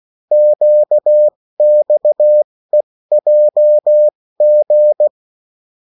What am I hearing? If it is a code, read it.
Morse code, heard as a keyed tone, about 16 wpm, spaced as for 14 wpm, 600 Hz. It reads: QXEJG
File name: morse mp3 / revision9-vitesse 14 mots